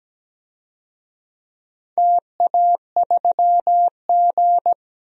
Morse code, heard as a keyed tone, about 17 wpm, 700 Hz. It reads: TA3G
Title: DXCC-100 Call Signs - 4 Characters - 17wpm